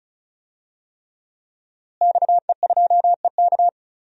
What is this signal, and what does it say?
Morse code, heard as a keyed tone, about 36 wpm, 700 Hz. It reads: XE2EK
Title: DXCC-100 Call Signs - 5 Characters - Repeated 3x - 35wpm